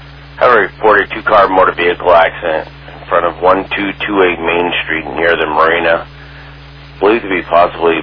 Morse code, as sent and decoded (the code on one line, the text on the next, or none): none